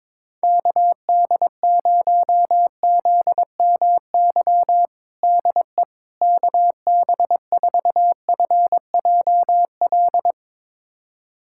Morse code, heard as a keyed tone, about 22 wpm, 700 Hz. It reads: KD0ZMY DE KB4FJL
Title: USA Callsign DE Callsign - 6 Characters - 22wpm